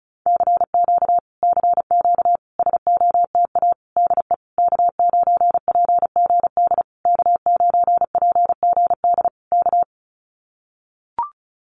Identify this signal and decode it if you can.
Morse code, heard as a keyed tone, about 34 wpm, 700 Hz. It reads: CQ CQ SOTA DE K9PGD K9PGD K
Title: Common QSO Phrases Encore 35wpm